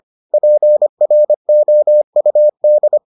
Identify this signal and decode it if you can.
Morse code, heard as a keyed tone, about 25 wpm, 600 Hz. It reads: PROUD